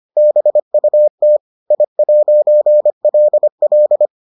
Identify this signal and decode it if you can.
Morse code, heard as a keyed tone, about 25 wpm, 600 Hz. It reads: BUT I'LL